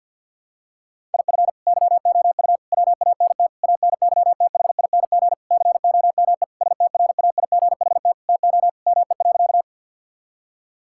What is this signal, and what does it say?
Morse code, heard as a keyed tone, about 50 wpm, 700 Hz. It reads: IF YOU WANT ANYTHING COME STRAIGHT TO ME.
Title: Sentences from Top 1000 Words - Review Part 6 - 50wpm